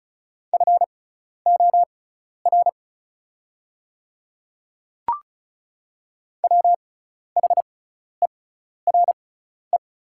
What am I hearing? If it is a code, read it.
Morse code, heard as a keyed tone, about 36 wpm, spaced as for 12 wpm, 690 Hz. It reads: FOR WHERE
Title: 6x Character Spacing - Top 200 Words - 35wpm